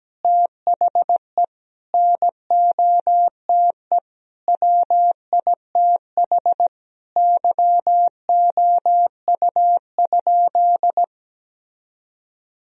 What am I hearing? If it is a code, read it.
Morse code, heard as a keyed tone, about 17 wpm, 700 Hz. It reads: THE NOTE WITH YOU?
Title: Sentences from Top 1000 Words - Review Part 2 - 17wpm